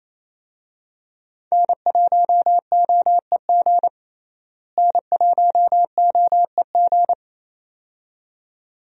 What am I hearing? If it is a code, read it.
Morse code, heard as a keyed tone, about 28 wpm, 700 Hz. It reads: N1OEG N1OEG
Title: USA Call Signs - 5 Characters - Repeated 2x - 28wpm